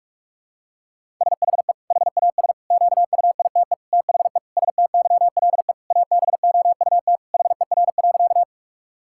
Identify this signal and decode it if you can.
Morse code, heard as a keyed tone, about 46 wpm, 700 Hz. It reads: SHE HAS QUITE THE STYLE ABOUT HER.
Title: Sentences from Top 1000 Words - Review Part 5 - 45wpm